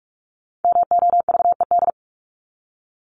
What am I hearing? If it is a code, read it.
Morse code, heard as a keyed tone, about 44 wpm, 700 Hz. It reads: MOVED
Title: ED Suffix - Top 100 Words with Base Words 45wpm